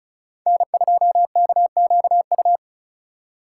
Morse code, heard as a keyed tone, about 44 wpm, 700 Hz. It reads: N2KQU
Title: Speed Racing - USA 1x3 Call Signs 35wpm